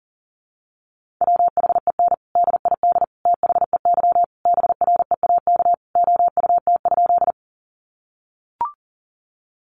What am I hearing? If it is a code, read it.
Morse code, heard as a keyed tone, about 37 wpm, 700 Hz. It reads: WHEN DID THEY BREAK OUT?
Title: Sentences from Top 900 Words 40wpm